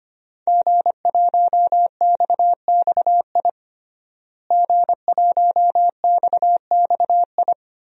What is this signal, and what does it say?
Morse code, heard as a keyed tone, about 25 wpm, 700 Hz. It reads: G1XXI G1XXI
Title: DXCC-100 Call Signs - 5 Characters - Repeated 2x - 25wpm